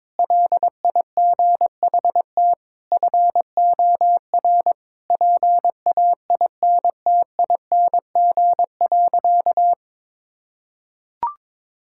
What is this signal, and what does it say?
Morse code, heard as a keyed tone, about 22 wpm, 700 Hz. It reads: LIGHT FOR PAINTING.
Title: Sentences from Top 1000 Words - Review Part 6 - 22wpm